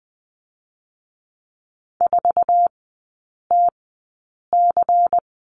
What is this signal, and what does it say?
Morse code, heard as a keyed tone, about 20 wpm, spaced as for 8 wpm, 700 Hz. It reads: 4TC
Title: 184 Lesson 30e - Letter b - Three Characters 20wpm